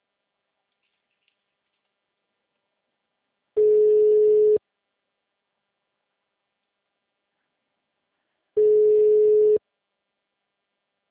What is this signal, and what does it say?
Ringback tone (Europe, 425 Hz).